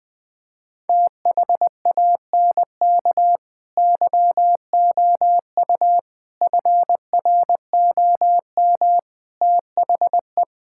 Morse code, heard as a keyed tone, about 21 wpm, 700 Hz. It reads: THANK YOU FROM THE
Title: Sets of 8 Words - 20wpm